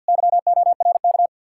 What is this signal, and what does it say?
Morse code, heard as a keyed tone, about 67 wpm, spaced as for 58 wpm, 700 Hz. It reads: YORK
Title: Speed-Racing Common QSO Phrases - 50wpm